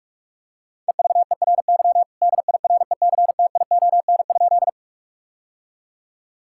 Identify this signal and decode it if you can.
Morse code, heard as a keyed tone, about 46 wpm, 700 Hz. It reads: EVERY DIRECTION?
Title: Sentences from Top 1000 Words - Review Part 5 - 45wpm